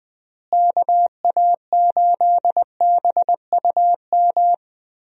Morse code, heard as a keyed tone, about 20 wpm, 700 Hz. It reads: KA8BUM
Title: USA Call Signs - 6 Characters - Repeated 3x - 20wpm